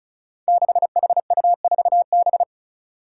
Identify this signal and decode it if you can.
Morse code, heard as a keyed tone, about 36 wpm, 700 Hz. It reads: 6HU4B
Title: Five Challenging Letter-Number Combinations 35wpm